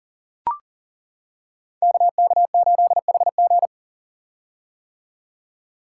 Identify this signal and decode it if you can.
Morse code, heard as a keyed tone, about 42 wpm, 700 Hz. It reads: KK8HG
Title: Code Groups - 5 Letter:Numbers - 40wpm